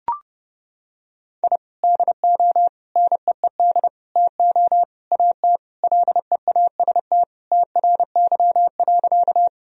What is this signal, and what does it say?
Morse code, heard as a keyed tone, about 30 wpm, 700 Hz. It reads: I DO NEED TO AT LEAST TRY.
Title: Sentences from Top 1000 Words - Review Part 1 - 30wpm